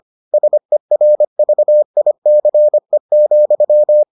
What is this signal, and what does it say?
Morse code, heard as a keyed tone, about 25 wpm, 600 Hz. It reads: SERVICE,